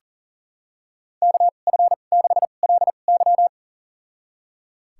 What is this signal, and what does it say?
Morse code, heard as a keyed tone, about 40 wpm, spaced as for 29 wpm, 700 Hz. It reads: KF6LY
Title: USA Call Signs - 5 Characters - 2x Character Spacing - 40wpm